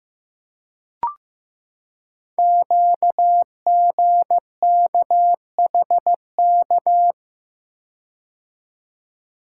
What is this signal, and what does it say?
Morse code, heard as a keyed tone, about 15 wpm, 700 Hz. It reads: QGKHK
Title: Code Groups - 5 Letters - 15wpm